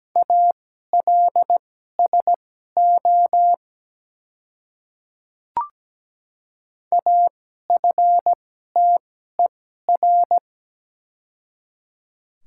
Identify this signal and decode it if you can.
Morse code, heard as a keyed tone, about 17 wpm, spaced as for 12 wpm, 700 Hz. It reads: ALSO AFTER
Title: 2x Character Spacing - Top 200 Words - 17wpm